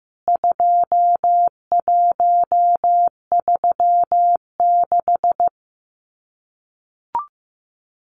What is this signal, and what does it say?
Morse code, heard as a keyed tone, about 15 wpm, 700 Hz. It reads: 2136
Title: CWT Exchange 15wpm